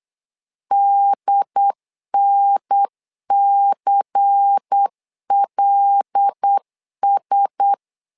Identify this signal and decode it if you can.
Morse code, heard as a keyed tone, about 8 wpm, 800 Hz. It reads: DNCLS
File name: LEZIONI / 5 LBVC / 5 LBVC 8wpm 2935